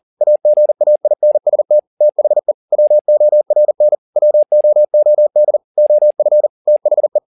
Morse code, heard as a keyed tone, about 40 wpm, 600 Hz. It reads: AGAINST THE WORN WOOD OF THE
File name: the old man and the sea 2 40wpm